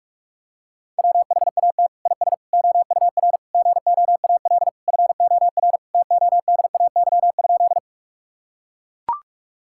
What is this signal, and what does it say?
Morse code, heard as a keyed tone, about 41 wpm, 700 Hz. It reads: WHAT IS OUR GOAL FOR TODAY?
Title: Sentences from Top 1000 Words - Review Part 5 - 45wpm